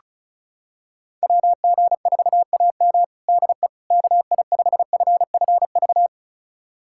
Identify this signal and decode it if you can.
Morse code, heard as a keyed tone, about 36 wpm, 700 Hz. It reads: WG4AM DE KI5FFV
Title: USA Callsign DE Callsign 35wpm